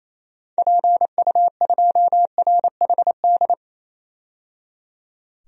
Morse code, heard as a keyed tone, about 28 wpm, 700 Hz. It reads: PU2RHD
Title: DXCC-100 Call Signs - 6 Characters - 28wpm